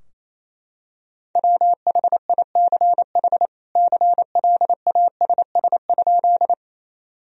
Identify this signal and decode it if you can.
Morse code, heard as a keyed tone, about 28 wpm, 700 Hz. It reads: WHICH CLASS?
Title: Sentences from Top 1000 Words - Review Part 2 - 28wpm